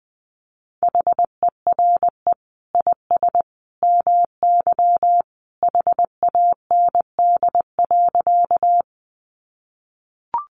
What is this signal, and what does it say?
Morse code, heard as a keyed tone, about 20 wpm, 700 Hz. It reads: HERE IS MY HAND.
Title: Sentences from Top 200 Words 20wpm